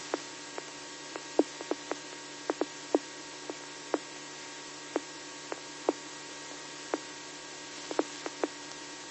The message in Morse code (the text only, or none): none